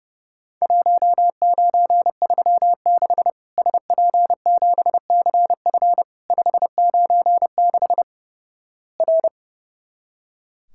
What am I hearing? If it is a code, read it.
Morse code, heard as a keyed tone, about 30 wpm, 700 Hz. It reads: 1936 SP7CF 596 R
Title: Worked All Europe DX Contest - QTC Exchange - 30wpm